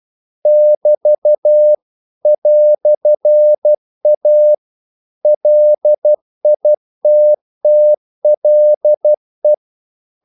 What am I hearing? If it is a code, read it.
Morse code, heard as a keyed tone, about 12 wpm, 600 Hz. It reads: <BT> "A LITTLE
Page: English quotations in CW (telegraphy - morse)